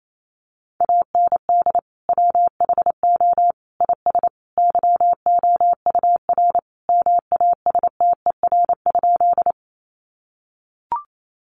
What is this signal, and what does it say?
Morse code, heard as a keyed tone, about 28 wpm, 700 Hz. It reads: AND WHO IS YOUR MASTER?
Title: Sentences from Top 1300 Words 28wpm